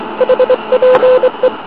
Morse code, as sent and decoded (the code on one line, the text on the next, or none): .... .--. .
HPE